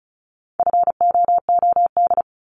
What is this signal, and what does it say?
Morse code, heard as a keyed tone, about 36 wpm, 700 Hz. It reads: FOOD
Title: Multi-letter Phonemes 35wpm